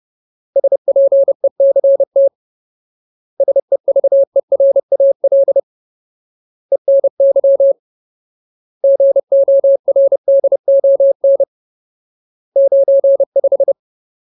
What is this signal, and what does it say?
Morse code, heard as a keyed tone, about 30 wpm, 550 Hz. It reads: SPECT SEVERAL ENY GORDON 95
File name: Ultimate Word List / 30WPM Ultimate